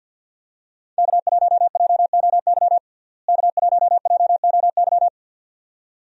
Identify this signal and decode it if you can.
Morse code, heard as a keyed tone, about 47 wpm, 700 Hz. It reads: K1JOY K1JOY
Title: USA Call Signs - 5 Characters - Repeated 2x - 50wpm